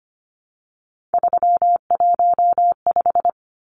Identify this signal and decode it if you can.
Morse code, heard as a keyed tone, about 25 wpm, 700 Hz. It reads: 315
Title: Numbers - 3-Digit 25wpm